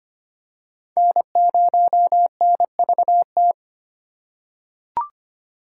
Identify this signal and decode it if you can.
Morse code, heard as a keyed tone, about 25 wpm, 700 Hz. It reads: N0NVT
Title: USA Call Signs - 5 Characters - 2x Character Spacing - 25wpm